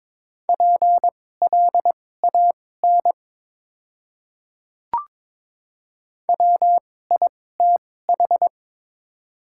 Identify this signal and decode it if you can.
Morse code, heard as a keyed tone, about 22 wpm, spaced as for 16 wpm, 700 Hz. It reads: PLAN WITH
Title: 2x Character Spacing - Top 500 Words - 22wpm